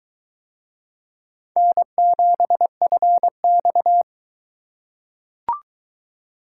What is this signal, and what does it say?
Morse code, heard as a keyed tone, about 21 wpm, 700 Hz. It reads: N7FX